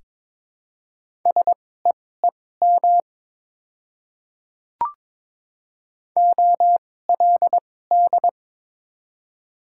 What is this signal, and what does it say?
Morse code, heard as a keyed tone, about 22 wpm, spaced as for 16 wpm, 700 Hz. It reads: SEEM OLD
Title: 2x Character Spacing - Top 300 Words - 22wpm